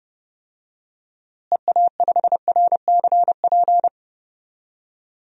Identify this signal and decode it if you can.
Morse code, heard as a keyed tone, about 30 wpm, 700 Hz. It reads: EA5RCP
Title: DXCC-100 Call Signs - 6 Characters - Repeated 3x - 30wpm